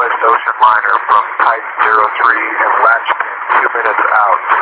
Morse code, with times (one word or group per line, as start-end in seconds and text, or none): none